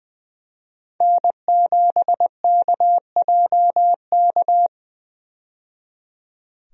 Morse code, heard as a keyed tone, about 20 wpm, 700 Hz. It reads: N7KJK
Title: USA Call Signs - 5 Characters - 20wpm